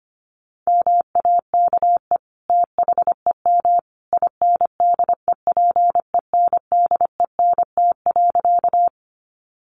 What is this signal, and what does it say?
Morse code, heard as a keyed tone, about 25 wpm, 700 Hz. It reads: MAKE THEM INDEPENDENT.
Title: Sentences from Top 900 Words 25wpm